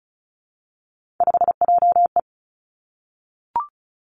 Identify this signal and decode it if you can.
Morse code, heard as a keyed tone, about 36 wpm, 700 Hz. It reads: 5JE
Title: Three Challenging Letter-Number Combinations 35wpm